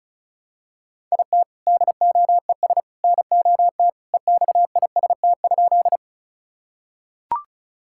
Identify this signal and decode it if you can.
Morse code, heard as a keyed tone, about 34 wpm, 700 Hz. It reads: IT DOES NOT EXIST?
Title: Sentences from Top 1000 Words - Review Part 5 - 35wpm